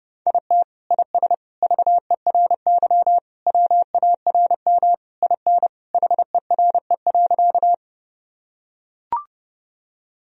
Morse code, heard as a keyed tone, about 30 wpm, 700 Hz. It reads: IT IS VERY WARM IN HERE.
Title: Sentences from Top 1000 Words - Review Part 2 - 30wpm